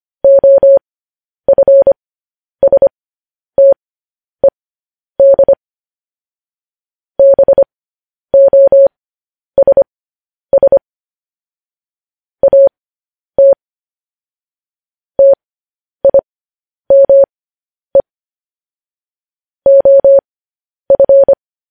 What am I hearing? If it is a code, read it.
Morse code, heard as a keyed tone, about 25 wpm, spaced as for 10 wpm, 550 Hz. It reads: OFSTED BOSS AT TIME OF
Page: news-headlines-in-morse-code-at-10-wpm